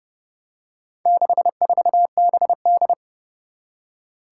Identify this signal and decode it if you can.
Morse code, heard as a keyed tone, about 30 wpm, 700 Hz. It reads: 64BD
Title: Four Challenging Letter-Number Combinations 30wpm